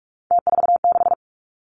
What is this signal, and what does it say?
Morse code, heard as a keyed tone, about 46 wpm, 700 Hz. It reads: T46
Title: Three Challenging Letter-Number Combinations 45wpm